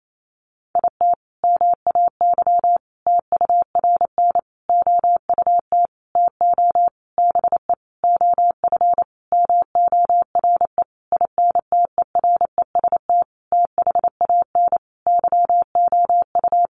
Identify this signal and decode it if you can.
Morse code, heard as a keyed tone, about 28 wpm, 700 Hz. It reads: IT MAY TURN OUT TO BE OF MORE INTEREST THAN YOU
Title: Sentences from Top 400 Words 28wpm